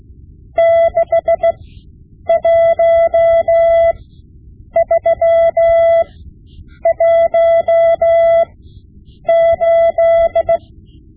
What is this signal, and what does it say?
Morse code, heard as a keyed tone, about 16 wpm, spaced as for 8 wpm, 670 Hz. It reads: 61318
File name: Morse Lessons / Morse Lessons / Tape-5B